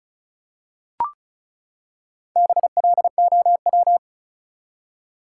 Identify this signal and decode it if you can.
Morse code, heard as a keyed tone, about 36 wpm, 700 Hz. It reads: BLOW
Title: Four Letter Words - 35wpm